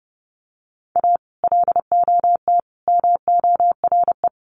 Morse code, heard as a keyed tone, about 30 wpm, 700 Hz. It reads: A LOT MORE
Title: Sets of 3 Words 30wpm